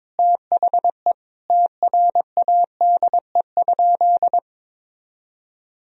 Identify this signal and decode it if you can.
Morse code, heard as a keyed tone, about 22 wpm, 700 Hz. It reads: THE TRADE?
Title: Sentences from Top 1000 Words - Review Part 4 - 22wpm